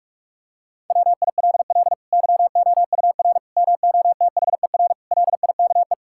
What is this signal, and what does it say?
Morse code, heard as a keyed tone, about 45 wpm, 700 Hz. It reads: WILL YOUR MOTHER LIKE
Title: Sentences from Top 1000 Words - Review Part 5 - 45wpm